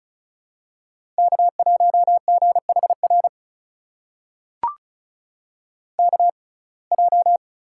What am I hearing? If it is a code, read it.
Morse code, heard as a keyed tone, about 36 wpm, spaced as for 25 wpm, 700 Hz. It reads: K1GHR KJ
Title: USA Call Signs - 5 Characters - 6x Character Spacing - 35wpm